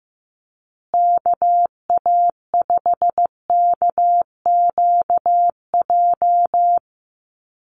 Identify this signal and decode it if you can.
Morse code, heard as a keyed tone, about 15 wpm, 700 Hz. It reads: KA5KQJ
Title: USA Call Signs Repeated 2x 15wpm